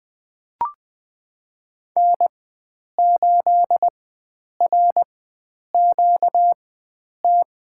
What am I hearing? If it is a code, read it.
Morse code, heard as a keyed tone, about 20 wpm, spaced as for 9 wpm, 700 Hz. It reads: N8RQT
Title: USA Call Signs - 5 Characters - 4x Character Spacing - 20wpm